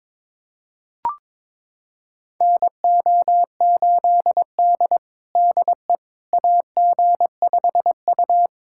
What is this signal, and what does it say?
Morse code, heard as a keyed tone, about 22 wpm, 700 Hz. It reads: NO8D DE AG5U
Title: USA Callsign DE Callsign - 4 Characters - 22wpm